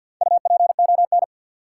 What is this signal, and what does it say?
Morse code, heard as a keyed tone, about 52 wpm, 700 Hz. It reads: UPON
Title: Sentences from Top 1100 Words - New Word Drill Alt - 50wpm